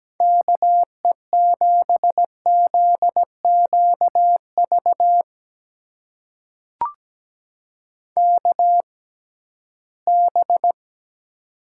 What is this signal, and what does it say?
Morse code, heard as a keyed tone, about 17 wpm, spaced as for 13 wpm, 700 Hz. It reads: KE7ZQV KB
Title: USA Call Signs - 6 Characters - 6x Character Spacing - 17wpm